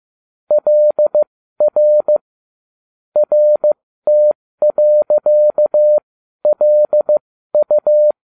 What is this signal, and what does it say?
Morse code, heard as a keyed tone, about 15 wpm, spaced as for 12 wpm, 600 Hz. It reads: LR RT.LU